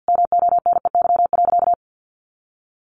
Morse code, heard as a keyed tone, about 50 wpm, 700 Hz. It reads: MONEY.
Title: Easy Phrases 50wpm